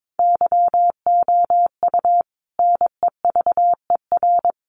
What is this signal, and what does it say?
Morse code, heard as a keyed tone, about 22 wpm, 700 Hz. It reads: YOU NEVER
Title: Sentences from Top 1300 Words 22wpm